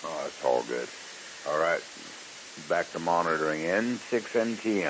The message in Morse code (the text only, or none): none